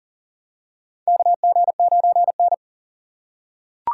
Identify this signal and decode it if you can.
Morse code, heard as a keyed tone, about 40 wpm, 700 Hz. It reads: KG9N
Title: USA Call Signs - 4 Characters - 6x Character Spacing - 40wpm